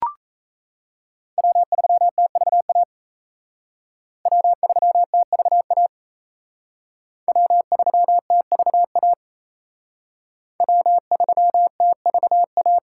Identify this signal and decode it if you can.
Morse code, heard as a keyed tone, about 35 wpm, 700 Hz. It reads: W3TVA W3TVA W3TVA W3TVA